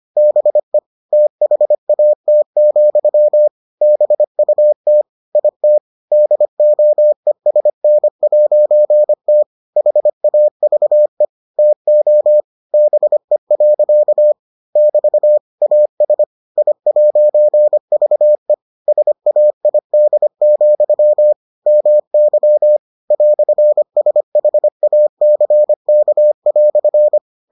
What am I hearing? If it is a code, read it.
Morse code, heard as a keyed tone, about 25 wpm, 600 Hz. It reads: BE THAT, BUT IT DOESN'T HAVE TO BE. <BT> AS I'VE SAID, MY "SHACK"